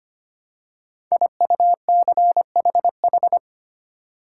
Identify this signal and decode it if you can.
Morse code, heard as a keyed tone, about 25 wpm, 700 Hz. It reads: IUCHH
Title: Code Groups - 5 Letters - 25wpm